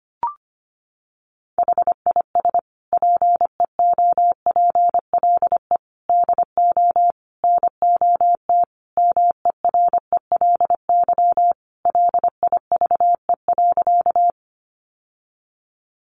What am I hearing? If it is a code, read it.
Morse code, heard as a keyed tone, about 25 wpm, 700 Hz. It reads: HIS PEOPLE DO NOT MERELY LIVE.
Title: Sentences from Top 1400 Words 25wpm